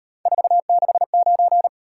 Morse code, heard as a keyed tone, about 36 wpm, 700 Hz. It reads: 469